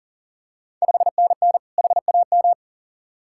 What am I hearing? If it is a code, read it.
Morse code, heard as a keyed tone, about 39 wpm, 700 Hz. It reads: 5NN HAM
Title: Ontario QSO Party Exchange 40wpm